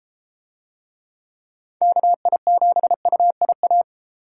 Morse code, heard as a keyed tone, about 30 wpm, 700 Hz. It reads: KI7UIA